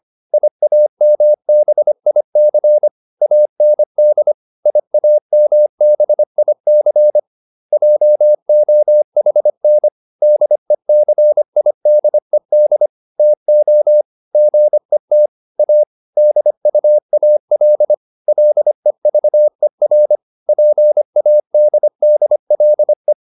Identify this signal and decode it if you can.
Morse code, heard as a keyed tone, about 25 wpm, 600 Hz. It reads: IAMBIC AND IAMBIC JOHN DECIDED TO GET A DUAL LEVER PADDLE